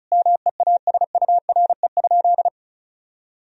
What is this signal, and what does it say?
Morse code, heard as a keyed tone, about 36 wpm, 700 Hz. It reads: MEASURE?
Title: Sentences from Top 1000 Words - Review Part 2 - 35wpm